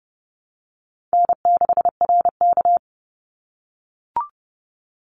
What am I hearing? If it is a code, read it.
Morse code, heard as a keyed tone, about 32 wpm, 700 Hz. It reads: N6RK
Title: Easy Call Signs - 30wpm